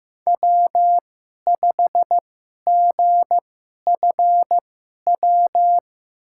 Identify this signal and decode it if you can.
Morse code, heard as a keyed tone, about 15 wpm, spaced as for 11 wpm, 700 Hz. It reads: W5GFW
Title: USA Call Signs - 5 Characters - 2x Character Spacing - 15wpm